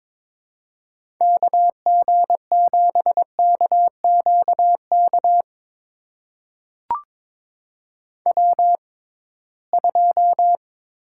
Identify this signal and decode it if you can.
Morse code, heard as a keyed tone, about 22 wpm, spaced as for 17 wpm, 700 Hz. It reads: KG7KQK W2
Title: USA Call Signs - 6x Character Spacing - 22wpm